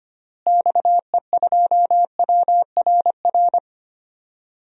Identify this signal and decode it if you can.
Morse code, heard as a keyed tone, about 25 wpm, 700 Hz. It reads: XE2WRR
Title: DXCC-100 Call Signs - 6 Characters - Repeated 2x - 25wpm